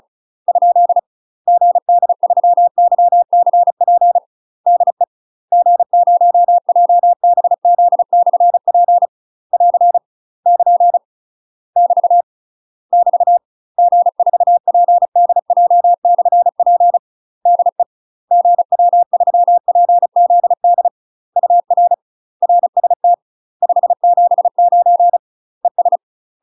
Morse code, heard as a keyed tone, about 36 wpm, 700 Hz. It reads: ? GD3YCP DE G0JBZ/P <AR> <KN> <BT> <BT> G4PDJ/P DE GW3PZD UR RST 579 ES
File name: UK MASTER QSOs TRAINING / 35WPM / UK MASTER QSOs 0022